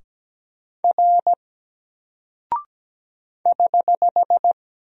Morse code, heard as a keyed tone, about 17 wpm, 700 Hz. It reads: R <HH>
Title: Single Character (Letter:Number:Punctuation:Prosign) 17wpm